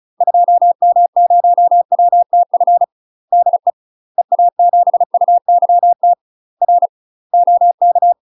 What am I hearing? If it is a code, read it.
Morse code, heard as a keyed tone, about 34 wpm, 700 Hz. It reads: 2M0WTF DE EA7UYT R OK
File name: UK MASTER QSOs TRAINING / 35WPM / UK MASTER QSOs 0005